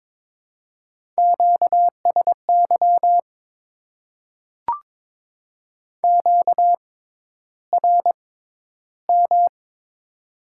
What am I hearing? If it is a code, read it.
Morse code, heard as a keyed tone, about 22 wpm, spaced as for 11 wpm, 700 Hz. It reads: QSY QRM
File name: Top 90 QSO Elements - 6x Character Spacing - 22wpm